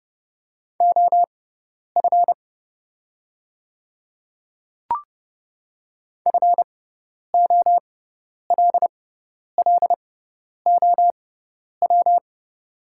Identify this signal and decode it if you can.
Morse code, heard as a keyed tone, about 30 wpm, spaced as for 10 wpm, 700 Hz. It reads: OF FOLLOW
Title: 6x Character Spacing - Top 300 Words - 30wpm